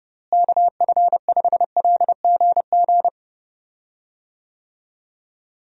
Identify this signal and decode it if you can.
Morse code, heard as a keyed tone, about 30 wpm, 700 Hz. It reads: KF5LGG
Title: USA Call Signs - 6 Characters - Repeated 2x - 30wpm